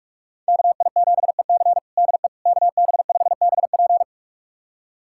Morse code, heard as a keyed tone, about 46 wpm, 700 Hz. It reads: KI7EC DE KB5BP
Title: USA Callsign DE Callsign - 5 Characters - 45wpm